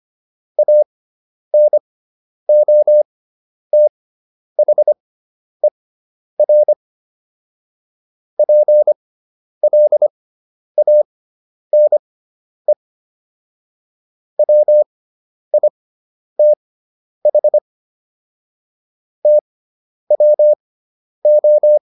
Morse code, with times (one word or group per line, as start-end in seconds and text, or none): 0.6-6.7 ANOTHER
8.4-12.7 PLANE
14.4-17.6 WITH
19.3-21.8 TWO